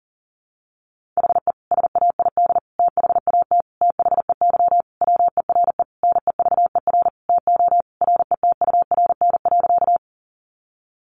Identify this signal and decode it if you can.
Morse code, heard as a keyed tone, about 40 wpm, 700 Hz. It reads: HE SAID THAT THEY WERE NEVER TO RETURN.
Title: Sentences from Top 800 Words 40wpm